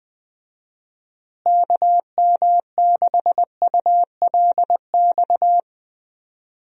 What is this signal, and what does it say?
Morse code, heard as a keyed tone, about 20 wpm, 700 Hz. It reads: KM6ULX
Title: USA Call Signs - 6 Characters - Repeated 2x - 20wpm